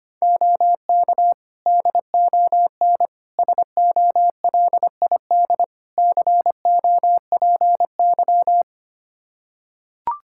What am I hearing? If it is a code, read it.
Morse code, heard as a keyed tone, about 25 wpm, 700 Hz. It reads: OK DON SOLID COPY
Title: Practice QSO - Exchange 3 - More Info - 25wpm